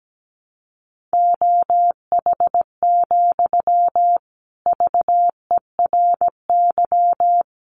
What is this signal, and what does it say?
Morse code, heard as a keyed tone, about 17 wpm, 700 Hz. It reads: OH, VERY
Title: Sentences from Top 1000 Words 17wpm